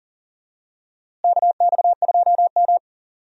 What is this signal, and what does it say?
Morse code, heard as a keyed tone, about 40 wpm, 700 Hz. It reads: KX2M
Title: USA Call Signs - 4 Characters - Repeated 2x - 40wpm